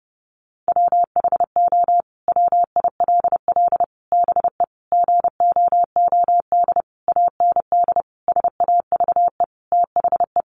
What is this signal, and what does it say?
Morse code, heard as a keyed tone, about 30 wpm, 700 Hz. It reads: WHO WILL BE GOOD AND SAVE THE
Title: Sentences from Top 1100 Words 30wpm